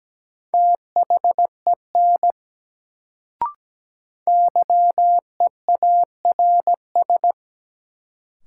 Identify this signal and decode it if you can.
Morse code, heard as a keyed tone, about 17 wpm, 700 Hz. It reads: THEN YEARS
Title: Sets of 2 Words - New Word Drill - 17wpm